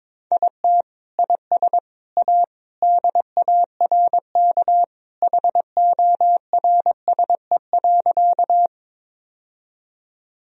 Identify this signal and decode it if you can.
Morse code, heard as a keyed tone, about 22 wpm, 700 Hz. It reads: IT IS A DARK HORSE.
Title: Sentences from Top 1000 Words - Review Part 1 - 22wpm